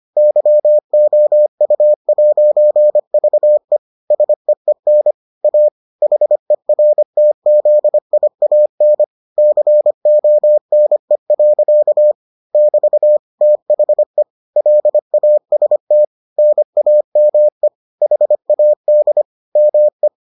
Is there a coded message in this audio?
Yes: Morse code from 0.2 to 20.1 s, heard as a keyed tone, about 25 wpm, 600 Hz. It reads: YOU'VE SEEN A HERTZIAN CONE. <BT> THE LAST NAME HAD ME